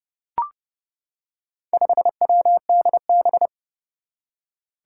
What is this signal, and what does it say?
Morse code, heard as a keyed tone, about 29 wpm, 700 Hz. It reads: 5WDB
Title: Four Challenging Letter-Number Combinations 30wpm